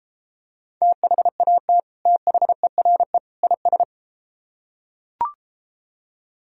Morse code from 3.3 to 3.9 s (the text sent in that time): IS